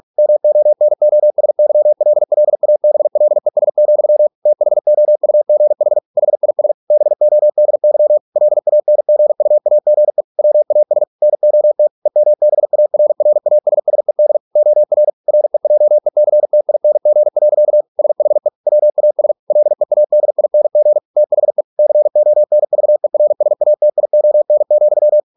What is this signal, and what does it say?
Morse code, heard as a keyed tone, about 46 wpm, 600 Hz. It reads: MONOSYLLABLES, THOUGH HIS BODY LANGUAGE WAS NOT EMBARRASSED OR REJECTING. SHE WAS LEADING THE CONVERSATION,